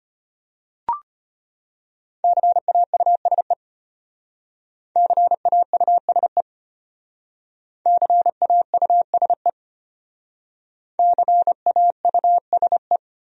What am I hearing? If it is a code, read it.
Morse code, heard as a keyed tone, about 34 wpm, 700 Hz. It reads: CAUSE CAUSE CAUSE CAUSE